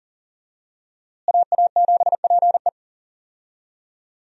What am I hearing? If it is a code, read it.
Morse code, heard as a keyed tone, about 37 wpm, 700 Hz. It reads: AA7PE
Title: USA Call Signs - 5 Characters - Repeated 3x - 40wpm